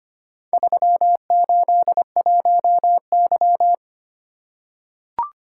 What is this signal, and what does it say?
Morse code, heard as a keyed tone, about 25 wpm, 700 Hz. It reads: 381Y
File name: practice / 25wpm / Four Letter-Number Combinations - 25wpm